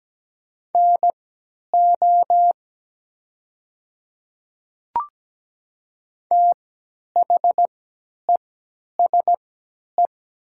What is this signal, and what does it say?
Morse code, heard as a keyed tone, about 17 wpm, spaced as for 10 wpm, 700 Hz. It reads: NO THESE
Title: 3x Character Spacing - Top 500 Words - 17wpm